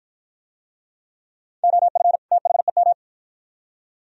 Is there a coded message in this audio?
Yes: Morse code from 1.6 to 2.9 s, heard as a keyed tone, about 48 wpm, 700 Hz. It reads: OF THEM